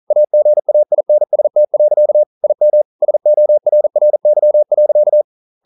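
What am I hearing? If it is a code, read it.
Morse code, heard as a keyed tone, about 42 wpm, 600 Hz. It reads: AGAINST. IM SORRY.
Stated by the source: Morse code practice at 41 wpm, from CW med Gnister.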